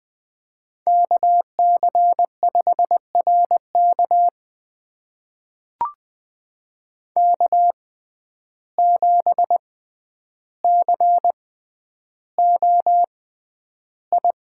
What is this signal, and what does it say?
Morse code, heard as a keyed tone, about 20 wpm, spaced as for 10 wpm, 700 Hz. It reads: KC5RK K7COI
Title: USA Call Signs - 5 Characters - 6x Character Spacing - 20wpm